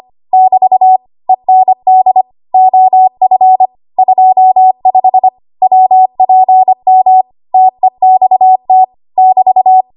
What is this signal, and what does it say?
Morse code, heard as a keyed tone, about 25 wpm, 750 Hz. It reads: <BT> END OF 25 WPM TEXT <BT>
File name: Archive / 25 WPM / 25WPM